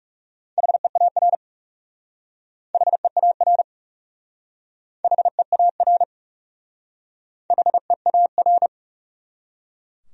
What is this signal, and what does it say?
Morse code, heard as a keyed tone, about 38 wpm, 700 Hz. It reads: HEAR HEAR HEAR HEAR